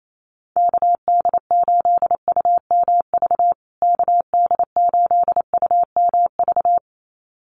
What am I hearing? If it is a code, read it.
Morse code, heard as a keyed tone, about 28 wpm, 700 Hz. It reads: KD8UMV KD8UMV
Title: USA Call Signs Repeated 2x 28wpm